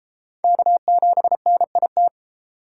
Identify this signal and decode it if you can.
Morse code, heard as a keyed tone, about 31 wpm, 700 Hz. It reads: K7NIT